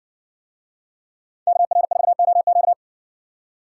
Morse code, heard as a keyed tone, about 60 wpm, 700 Hz. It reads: KR4OQ